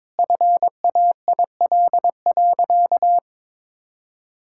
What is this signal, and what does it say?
Morse code, heard as a keyed tone, about 22 wpm, 700 Hz. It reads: FAIL.
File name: Sentences with 7-Word Phrases - 22wpm